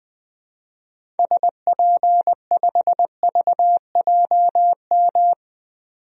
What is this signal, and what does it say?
Morse code, heard as a keyed tone, about 21 wpm, 700 Hz. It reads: SP5VJM
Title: DXCC-100 Call Signs - 6 Characters - 20wpm